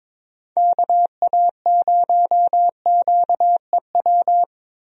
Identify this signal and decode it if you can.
Morse code, heard as a keyed tone, about 22 wpm, 700 Hz. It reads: KA0QEW
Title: USA Call Signs - 6 Characters - Repeated 2x - 22wpm